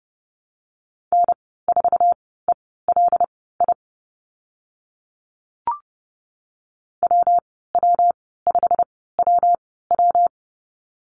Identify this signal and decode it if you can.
Morse code, heard as a keyed tone, about 30 wpm, spaced as for 17 wpm, 700 Hz. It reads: N4ELI WW5WW
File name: Lesson 19g - Letter w - Call Signs 30wpm